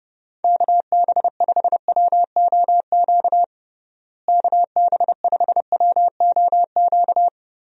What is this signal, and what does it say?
Morse code, heard as a keyed tone, about 30 wpm, 700 Hz. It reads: KB5WOQ KB5WOQ
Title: USA Call Signs - 6 Characters - Repeated 3x - 30wpm